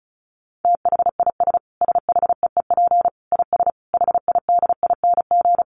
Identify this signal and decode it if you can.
Morse code, heard as a keyed tone, about 34 wpm, 700 Hz. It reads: THIS SHEEP IS HIDING
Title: Dit Challenge 35wpm